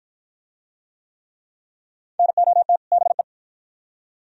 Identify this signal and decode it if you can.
Morse code, heard as a keyed tone, about 52 wpm, 700 Hz. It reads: NOT BE